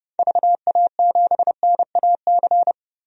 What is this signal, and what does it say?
Morse code, heard as a keyed tone, about 30 wpm, 700 Hz. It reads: VA7NAC
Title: DXCC-100 Call Signs - 6 Characters - 30wpm